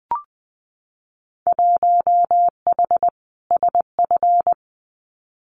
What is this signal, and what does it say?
Morse code, heard as a keyed tone, about 20 wpm, 700 Hz. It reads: 1H SF
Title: Winter Field Day Exchanges 20wpm